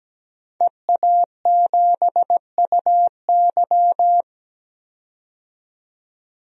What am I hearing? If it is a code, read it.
Morse code, heard as a keyed tone, about 17 wpm, 700 Hz. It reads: EA7UY
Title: DXCC-100 Call Signs - 5 Characters - 17wpm